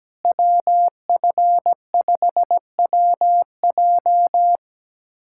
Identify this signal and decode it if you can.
Morse code, heard as a keyed tone, about 17 wpm, 700 Hz. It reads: WF5WJ
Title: K1USN SST QSO Practice - 17wpm